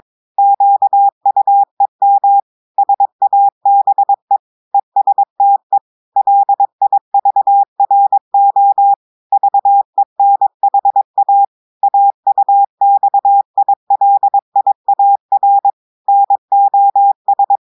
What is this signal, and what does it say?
Morse code, heard as a keyed tone, about 22 wpm, 800 Hz. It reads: QUEM SABE ESTE LIVRO VENHA AUXILIAR NOS
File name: audios / portugais / portugais texte en clair 1 vitesse 22 mots minutes